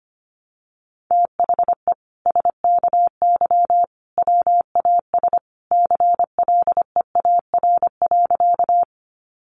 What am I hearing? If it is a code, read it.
Morse code, heard as a keyed tone, about 25 wpm, 700 Hz. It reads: THE SKY WAS CLEAR.
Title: Sentences from Top 1200 Words 25wpm